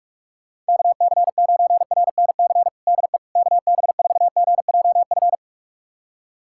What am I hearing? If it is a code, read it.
Morse code, heard as a keyed tone, about 46 wpm, 700 Hz. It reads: KC9RNC DE KB4GJF
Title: USA Callsign DE Callsign 45wpm